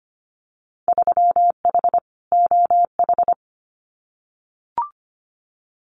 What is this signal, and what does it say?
Morse code, heard as a keyed tone, about 25 wpm, 700 Hz. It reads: 3H OH
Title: Winter Field Day Exchanges Encore 25wpm